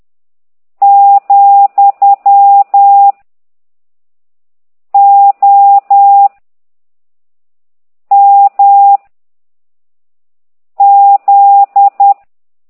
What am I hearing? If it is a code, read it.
Morse code, heard as a keyed tone, about 10 wpm, 800 Hz. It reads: , O M Z
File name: ct2idw / exerc15